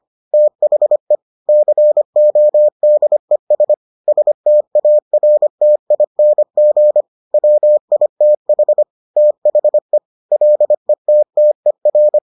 Morse code, heard as a keyed tone, about 25 wpm, 600 Hz. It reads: THE CODES STARTING WITH THE LETTER